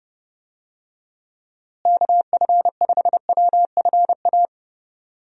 Morse code, heard as a keyed tone, about 30 wpm, 700 Hz. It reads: KF5WFA
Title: USA Call Signs - 6 Characters - Repeated 2x - 30wpm